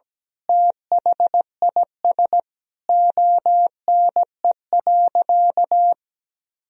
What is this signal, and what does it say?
Morse code, heard as a keyed tone, about 17 wpm, 700 Hz. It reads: THIS ONE.
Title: Sentences from Top 1000 Words - Review Part 4 - 17wpm